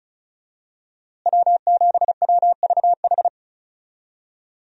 Morse code, heard as a keyed tone, about 34 wpm, 700 Hz. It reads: W7WVH
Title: USA Call Signs - 5 Characters - Repeated 2x - 35wpm